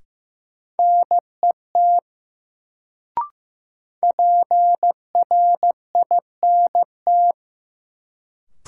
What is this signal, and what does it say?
Morse code, heard as a keyed tone, about 15 wpm, 700 Hz. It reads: NET PRINT
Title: Sentences from Top 1400 Words - New Word Drill - 15wpm